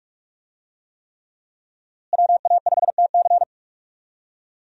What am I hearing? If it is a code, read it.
Morse code, heard as a keyed tone, about 41 wpm, 700 Hz. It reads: WA5TC
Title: USA Call Signs - 5 Characters - Repeated 3x - 45wpm